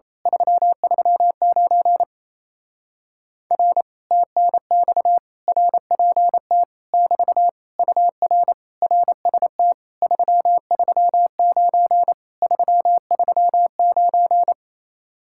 Morse code, heard as a keyed tone, about 27 wpm, 700 Hz. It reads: 339 R TNX RPT <BT> UR RST 339 339
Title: Speed-Racing Common QSO Phrases - 28wpm